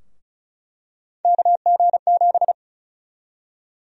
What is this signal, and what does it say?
Morse code, heard as a keyed tone, about 34 wpm, 700 Hz. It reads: KG7
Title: USA Call Sign Prefixes - 35wpm